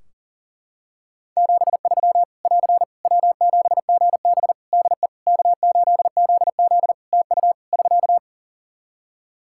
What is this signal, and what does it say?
Morse code, heard as a keyed tone, about 40 wpm, 700 Hz. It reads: 73 <AR> W7GB DE K8ZZ TU <SK>
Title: Practice QSO - Exchange 4 - Ending - 40wpm